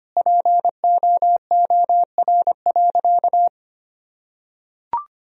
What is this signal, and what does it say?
Morse code, heard as a keyed tone, about 25 wpm, 700 Hz. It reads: POOR.
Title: Sentences from Top 1000 Words - Review Part 3 - 25wpm